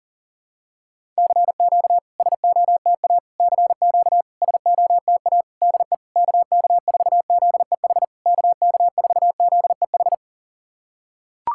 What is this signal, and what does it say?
Morse code, heard as a keyed tone, about 40 wpm, 690 Hz. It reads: CQ SOTA CQ SOTA DE KK4ZEH KK4ZEH
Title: SOTA QSO Practice 40wpm